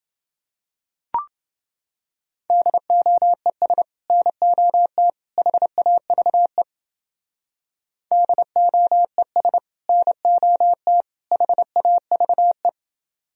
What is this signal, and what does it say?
Morse code, heard as a keyed tone, about 30 wpm, 700 Hz. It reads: DOES NOT HAVE DOES NOT HAVE